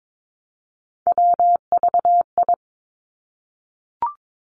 Morse code, heard as a keyed tone, about 22 wpm, 700 Hz. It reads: WVI
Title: Three Challenging Letter-Number Combinations 22wpm